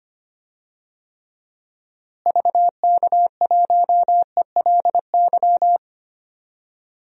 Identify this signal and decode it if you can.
Morse code, heard as a keyed tone, about 25 wpm, 700 Hz. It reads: VK1ELY
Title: DXCC-100 Call Signs - 6 Characters - Repeated 3x - 25wpm